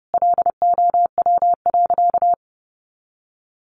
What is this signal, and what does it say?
Morse code, heard as a keyed tone, about 30 wpm, 700 Hz. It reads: LOW.
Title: Sentences from Top 1200 Words 30wpm